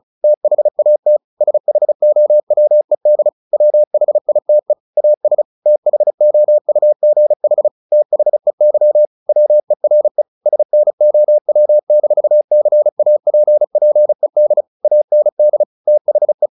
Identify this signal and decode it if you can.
Morse code, heard as a keyed tone, about 34 wpm, 600 Hz. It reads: THAT SHOWED WHITE AS THOUGH THEY WERE SNOW-CAPPED AND THE
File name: the old man and the sea / the old man and the sea 1 35wpm